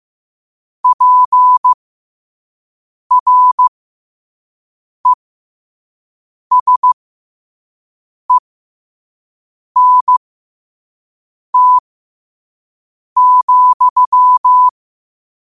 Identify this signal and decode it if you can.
Morse code, heard as a keyed tone, about 15 wpm, spaced as for 5 wpm, 1000 Hz. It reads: PRESENT,
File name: qotdfiles / 5wpm